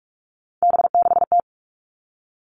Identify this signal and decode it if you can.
Morse code, heard as a keyed tone, about 46 wpm, 700 Hz. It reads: B6T
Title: Three Challenging Letter-Number Combinations 45wpm